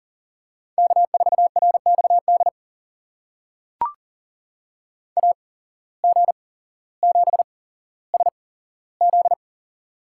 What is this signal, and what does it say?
Morse code, heard as a keyed tone, about 39 wpm, spaced as for 17 wpm, 700 Hz. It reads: K4RXD AG7SZ
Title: USA Call Signs - 5 Characters - 8x Character Spacing - 40wpm